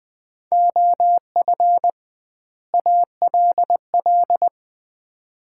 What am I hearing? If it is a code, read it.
Morse code, heard as a keyed tone, about 21 wpm, 700 Hz. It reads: OF ALL
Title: 2x Word-Spacing - Sets of 2 Words - 20wpm